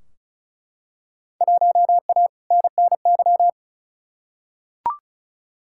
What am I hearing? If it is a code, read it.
Morse code, heard as a keyed tone, about 36 wpm, 700 Hz. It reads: 1A NNY
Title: Field Day Exchanges Encore - 35wpm